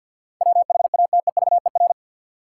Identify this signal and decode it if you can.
Morse code, heard as a keyed tone, about 52 wpm, 700 Hz. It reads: WHATEVER
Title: Sentences from Top 1100 Words - New Word Drill - 50wpm